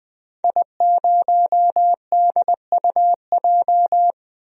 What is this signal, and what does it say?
Morse code, heard as a keyed tone, about 20 wpm, 700 Hz. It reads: I0DUJ
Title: DXCC-100 Call Signs - 5 Characters - Repeated 2x - 20wpm